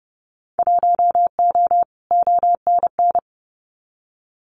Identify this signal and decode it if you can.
Morse code, heard as a keyed tone, about 30 wpm, 700 Hz. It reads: 1O ONN
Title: Winter Field Day Exchanges Encore 30wpm